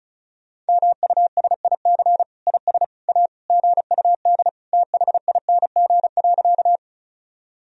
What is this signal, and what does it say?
Morse code, heard as a keyed tone, about 36 wpm, 700 Hz. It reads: MUSIC IS A GUD THING.
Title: Sentences from Top 300 Words and Abbreviations 35wpm